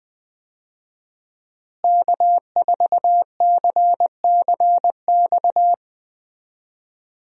Morse code, heard as a keyed tone, about 20 wpm, 700 Hz. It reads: K4CCX
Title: USA Call Signs - 5 Characters - Repeated 3x - 20wpm